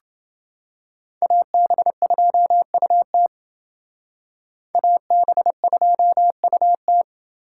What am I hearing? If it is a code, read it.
Morse code, heard as a keyed tone, about 29 wpm, 700 Hz. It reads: AB2UT AB2UT